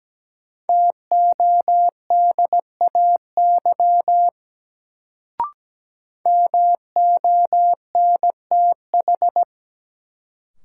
Morse code, heard as a keyed tone, about 17 wpm, 700 Hz. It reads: TODAY MONTH
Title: Sets of 5 Words Encore - New Word Drill - 17wpm